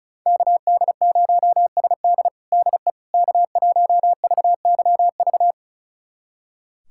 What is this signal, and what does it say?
Morse code, heard as a keyed tone, about 36 wpm, 700 Hz. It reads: KD0SD DE K1VYV
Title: USA Callsign DE Callsign 35wpm